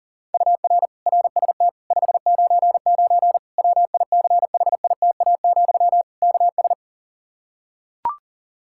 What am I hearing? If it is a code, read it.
Morse code, heard as a keyed tone, about 40 wpm, 700 Hz. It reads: UR RST 599 WICHITA, KS
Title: Practice QSO - Exchange 1 - Beginning - 40wpm